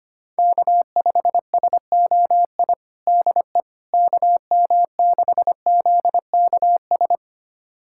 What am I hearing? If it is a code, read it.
Morse code, heard as a keyed tone, about 25 wpm, 700 Hz. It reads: K5SOI DE KM6ZKS
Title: USA Callsign DE Callsign 25wpm